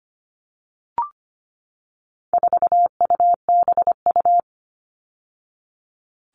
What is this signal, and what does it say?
Morse code, heard as a keyed tone, about 25 wpm, 700 Hz. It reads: 4UBU
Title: All Cut Numbers - 4-Digit 25wpm